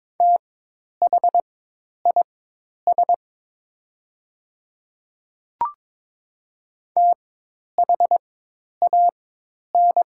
Morse code, heard as a keyed tone, about 22 wpm, spaced as for 10 wpm, 700 Hz. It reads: THIS THAN
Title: 4x Character Spacing - Top 100 Words - 22wpm